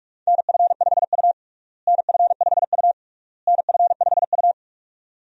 Morse code, heard as a keyed tone, about 44 wpm, 700 Hz. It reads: NF5U NF5U NF5U
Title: USA Call Signs - 4 Characters - Repeated 3x - 45wpm